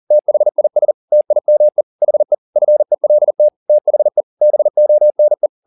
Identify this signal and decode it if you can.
Morse code, heard as a keyed tone, about 40 wpm, 600 Hz. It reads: THIS TIME HE FELT THE BONE
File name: the old man and the sea 2 40wpm